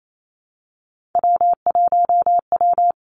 Morse code, heard as a keyed tone, about 28 wpm, 700 Hz. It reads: W1W
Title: Three Challenging Letter-Number Combinations 28wpm